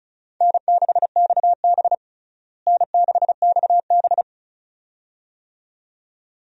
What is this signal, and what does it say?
Morse code, heard as a keyed tone, about 34 wpm, 700 Hz. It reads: N6XB N6XB
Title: USA Call Signs - 4 Characters - Repeated 2x - 35wpm